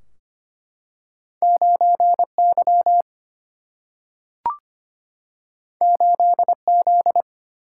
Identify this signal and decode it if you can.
Morse code, heard as a keyed tone, about 25 wpm, 700 Hz. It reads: 9Y 8Z
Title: Single-Letter Single-Number 25wpm